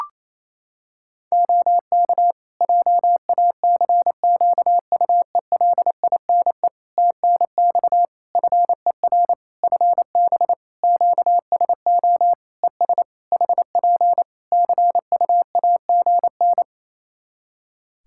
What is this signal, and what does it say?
Morse code, heard as a keyed tone, about 28 wpm, 700 Hz. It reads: OK JACQUELINE TNX FER FB QSO ES HP CUAGN
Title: K1USN SST QSO Practice 28wpm